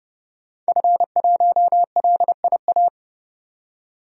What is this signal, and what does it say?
Morse code, heard as a keyed tone, about 30 wpm, 700 Hz. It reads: F1LIA
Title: DXCC-100 Call Signs - 5 Characters - 30wpm